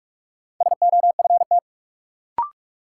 Morse code, heard as a keyed tone, about 46 wpm, 700 Hz. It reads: SOFT
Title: Sentences from Top 1500 Words - New Word Drill - 45wpm